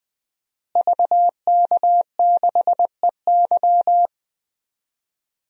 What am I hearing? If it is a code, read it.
Morse code, heard as a keyed tone, about 20 wpm, 700 Hz. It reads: VK6EY
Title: DXCC-100 Call Signs - 5 Characters - Repeated 2x - 20wpm